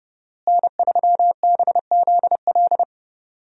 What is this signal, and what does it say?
Morse code, heard as a keyed tone, about 32 wpm, 700 Hz. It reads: N3BZL
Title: USA Call Signs - 5 Characters - Repeated 3x - 30wpm